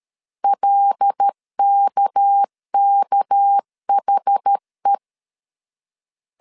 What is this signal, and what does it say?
Morse code, heard as a keyed tone, about 13 wpm, 800 Hz. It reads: LKKHE
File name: CW / DET / 1 6 1500C S A 12wpm 3440